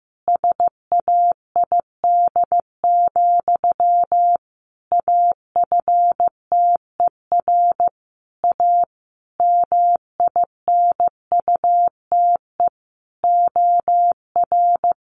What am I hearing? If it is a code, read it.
Morse code, heard as a keyed tone, about 15 wpm, 700 Hz. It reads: SAID, AFTER A MINUTE OR
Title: Sentences from Top 1400 Words 15wpm